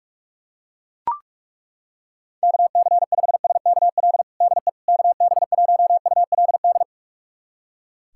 Morse code, heard as a keyed tone, about 41 wpm, 700 Hz. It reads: KC5SKL DE KB1ULD
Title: USA Callsign DE Callsign - 6 Characters - 45wpm